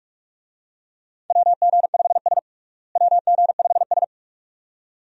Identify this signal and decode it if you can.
Morse code, heard as a keyed tone, about 46 wpm, 700 Hz. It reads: WG5S WG5S
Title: USA Call Signs - 4 Characters - Repeated 2x - 45wpm